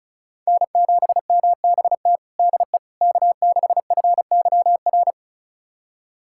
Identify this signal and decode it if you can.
Morse code, heard as a keyed tone, about 36 wpm, 700 Hz. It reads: N7MBT DE K6FYR
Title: USA Callsign DE Callsign - 5 Characters - 35wpm